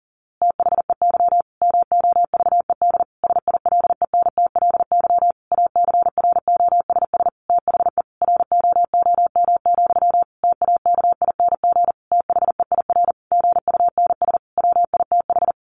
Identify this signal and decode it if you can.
Morse code, heard as a keyed tone, about 40 wpm, 700 Hz. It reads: THEY MOVED SILENTLY ACROSS THE ROOM, TAKING THEIR GUNS WITH
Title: A Sound Of Thunder 40wpm